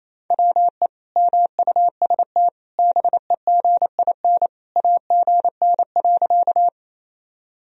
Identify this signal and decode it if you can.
Morse code, heard as a keyed tone, about 28 wpm, 700 Hz. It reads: WE MUST BEGIN AGN.
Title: Sentences from Top 300 Words and Abbreviations 28wpm